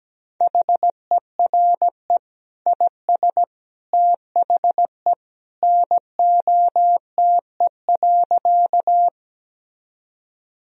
Morse code, heard as a keyed tone, about 17 wpm, 700 Hz. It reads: HERE IS THE NOTE.
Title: Sentences from Top 1000 Words - Review Part 2 - 17wpm